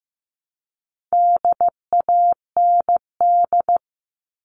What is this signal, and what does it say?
Morse code, heard as a keyed tone, about 15 wpm, 700 Hz. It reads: DAND
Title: All Cut Numbers - 4-Digit 15wpm